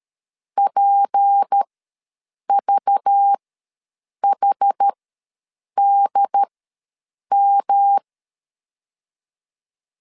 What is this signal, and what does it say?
Morse code, heard as a keyed tone, about 13 wpm, spaced as for 7 wpm, 800 Hz. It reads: PVHDM
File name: LEZIONI / 6 PKX0 / 6 P ws 12wpm 3117